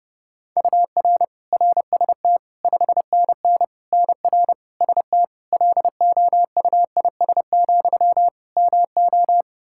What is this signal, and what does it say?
Morse code, heard as a keyed tone, about 30 wpm, 700 Hz. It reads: UR RST 5NN NR ST LOUIS, MO
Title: Practice QSO - Exchange 1 - Beginning - 30wpm